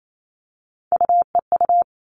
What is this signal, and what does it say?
Morse code, heard as a keyed tone, about 28 wpm, 700 Hz. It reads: UEU
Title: All Cut Numbers - 3-Digit 28wpm